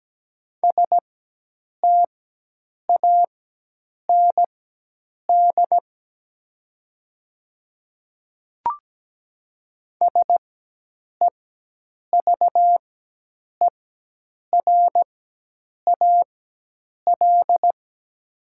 Morse code, heard as a keyed tone, about 17 wpm, spaced as for 8 wpm, 700 Hz. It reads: STAND SEVERAL
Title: 4x Character Spacing - Top 500 Words - 17wpm